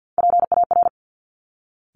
Morse code, heard as a keyed tone, about 50 wpm, 700 Hz. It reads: PAR
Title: Top Three Letter Combinations in Words 50wpm